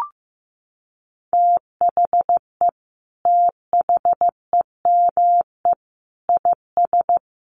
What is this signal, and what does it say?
Morse code, heard as a keyed tone, about 15 wpm, 700 Hz. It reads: THE THEME IS
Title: Sentences from Top 1300 Words 15wpm